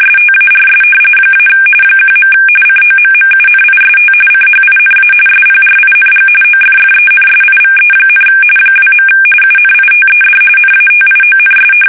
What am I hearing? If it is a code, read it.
RTTY, about 45.45 baud. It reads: STARS - 1. COMMUNIST CHINA SPRING OFFENSIVE, 2. UNITED NATIONS
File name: fsk850 / battleshipnj